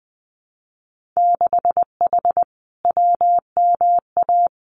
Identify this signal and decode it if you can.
Morse code, heard as a keyed tone, about 20 wpm, 700 Hz. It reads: 6H WMA
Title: Winter Field Day Exchanges Encore 20wpm